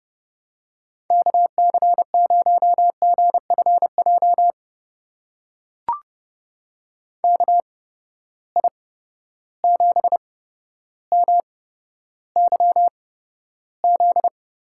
Morse code, heard as a keyed tone, about 32 wpm, spaced as for 13 wpm, 700 Hz. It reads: KC0GFJ KI7MYZ
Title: USA Call Signs - 6 Characters - 8x Character Spacing - 30wpm